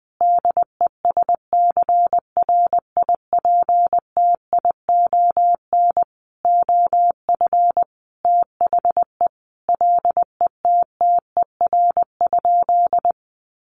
Morse code, heard as a keyed tone, about 20 wpm, 700 Hz. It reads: DESCRIPTION OF THE LETTER?
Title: Sentences from Top 1300 Words 20wpm